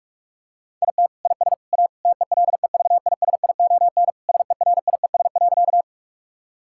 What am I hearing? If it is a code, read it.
Morse code, heard as a keyed tone, about 41 wpm, 700 Hz. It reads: IT IS A TELEVISION SERIES.
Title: Sentences from Top 1000 Words - Review Part 5 - 45wpm